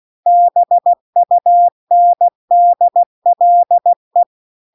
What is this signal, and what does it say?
Morse code, heard as a keyed tone, about 16 wpm, 700 Hz. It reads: BUNDLE